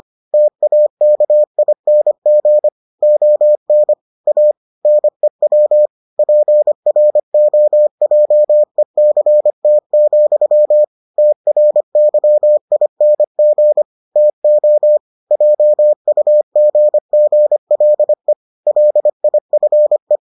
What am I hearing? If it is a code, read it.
Morse code, heard as a keyed tone, about 25 wpm, 600 Hz. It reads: TAKING ON A NEW PROJECT, TRYING TO JUGGLE LIFE